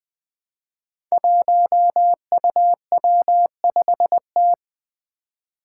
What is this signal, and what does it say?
Morse code, heard as a keyed tone, about 20 wpm, 700 Hz. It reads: 1UW5T
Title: Five Challenging Letter-Number Combinations 20wpm